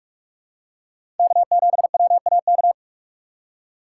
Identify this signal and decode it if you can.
Morse code, heard as a keyed tone, about 41 wpm, 700 Hz. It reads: K7WAK
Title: USA Call Signs - 5 Characters - 4x Character Spacing - 45wpm